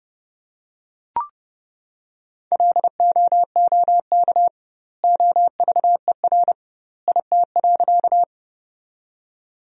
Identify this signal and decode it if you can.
Morse code, heard as a keyed tone, about 30 wpm, 700 Hz. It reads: LOOK OVER IT.
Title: 2x Word-Spacing - Sentences from Top 100 Words - 30wpm